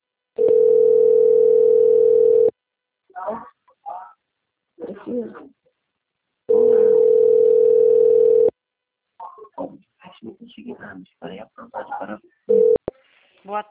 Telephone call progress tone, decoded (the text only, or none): ringback tone